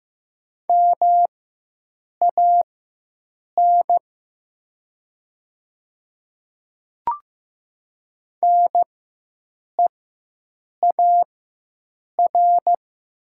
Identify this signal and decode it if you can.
Morse code, heard as a keyed tone, about 15 wpm, spaced as for 7 wpm, 700 Hz. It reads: MAN NEAR
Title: 4x Character Spacing - Top 500 Words - 15wpm